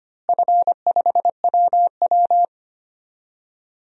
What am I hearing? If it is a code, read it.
Morse code, heard as a keyed tone, about 25 wpm, 700 Hz. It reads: F5WW
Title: DXCC-100 Call Signs - 4 Characters - Repeated 3x - 25wpm